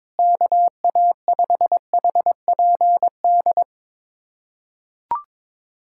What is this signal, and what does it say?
Morse code, heard as a keyed tone, about 21 wpm, 700 Hz. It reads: KA5HPD
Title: USA Callsign DE Callsign - 6 Characters - 22wpm